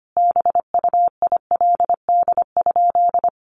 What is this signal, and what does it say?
Morse code, heard as a keyed tone, about 25 wpm, 700 Hz. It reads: BUILD?
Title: Sentences from Top 1300 Words 25wpm